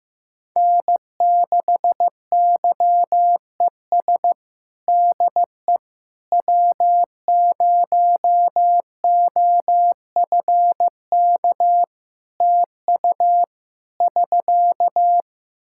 Morse code, heard as a keyed tone, about 15 wpm, 700 Hz. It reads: N6YES DE W0OFK TU <SK>
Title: K1USN SST QSO Practice 15wpm